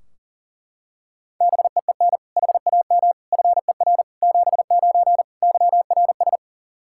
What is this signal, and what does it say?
Morse code, heard as a keyed tone, about 40 wpm, 700 Hz. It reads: BEEN HAM FER 79 YRS
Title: Practice QSO - Exchange 3 - More Info - 40wpm